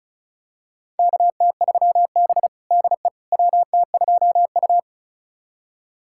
Morse code, heard as a keyed tone, about 36 wpm, 700 Hz. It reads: KT3B DE WT2U
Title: USA Callsign DE Callsign - 4 Characters - 35wpm